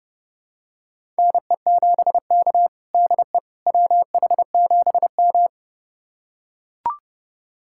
Morse code, heard as a keyed tone, about 32 wpm, 700 Hz. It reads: NE7K DE WH7M
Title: USA Callsign DE Callsign - 4 Characters - 30wpm